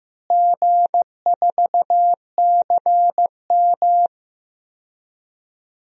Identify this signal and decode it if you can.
Morse code, heard as a keyed tone, about 16 wpm, 700 Hz. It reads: G4CM
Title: DXCC-100 Call Signs - 4 Characters - Repeated 3x - 15wpm